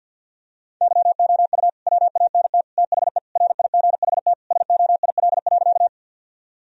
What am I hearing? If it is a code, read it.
Morse code, heard as a keyed tone, about 50 wpm, 700 Hz. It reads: YOU WANT THE RIGHT SOIL.
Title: Sentences from Top 1000 Words - Review Part 4 - 50wpm